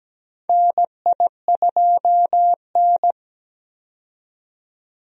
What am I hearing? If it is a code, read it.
Morse code, heard as a keyed tone, about 17 wpm, 700 Hz. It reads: NI2N
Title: USA Call Signs - 4 Characters - 17wpm